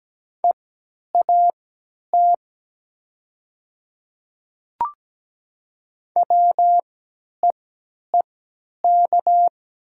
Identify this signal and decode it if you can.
Morse code, heard as a keyed tone, about 17 wpm, spaced as for 10 wpm, 700 Hz. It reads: EAT WEEK
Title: 3x Character Spacing - Top 500 Words - 17wpm